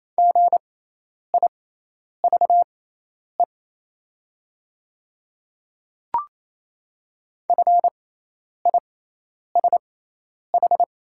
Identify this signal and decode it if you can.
Morse code, heard as a keyed tone, about 28 wpm, spaced as for 10 wpm, 700 Hz. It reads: GIVE FISH
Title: 6x Character Spacing - Top 300 Words - 28wpm